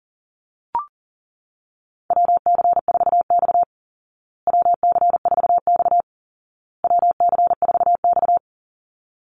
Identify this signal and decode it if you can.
Morse code, heard as a keyed tone, about 40 wpm, 700 Hz. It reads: WC4X WC4X WC4X
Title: Call Signs Repeated 2x - 40wpm